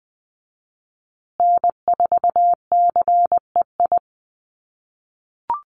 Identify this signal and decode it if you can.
Morse code, heard as a keyed tone, about 20 wpm, 700 Hz. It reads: N4CEI
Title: USA Call Signs Repeated 2x 20wpm